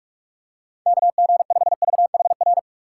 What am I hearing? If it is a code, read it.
Morse code, heard as a keyed tone, about 46 wpm, 700 Hz. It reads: KG5VHR
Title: USA Call Signs - 6 Characters - 3x Character Spacing - 45wpm